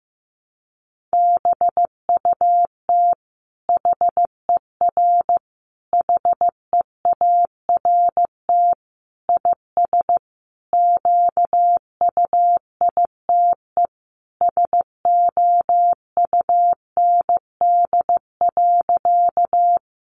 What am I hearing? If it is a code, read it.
Morse code, heard as a keyed tone, about 15 wpm, 700 Hz. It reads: BUT HER HEART IS QUITE SOUND.
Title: Sentences from Top 1200 Words 15wpm